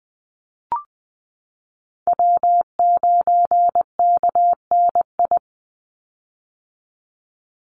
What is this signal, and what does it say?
Morse code, heard as a keyed tone, about 19 wpm, 700 Hz. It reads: W9KNI
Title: Call Signs - 20wpm